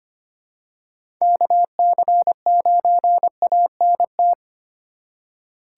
Morse code, heard as a keyed tone, about 25 wpm, 700 Hz. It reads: KC9ANT
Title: USA Call Signs - 6 Characters - 2x Character Spacing - 25wpm